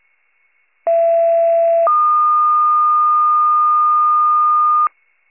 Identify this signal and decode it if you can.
Two-tone page, heard as a one-second tone, then a three-second tone, about 670 then 1220 Hz. It is heard from 0.9 to 4.9 s.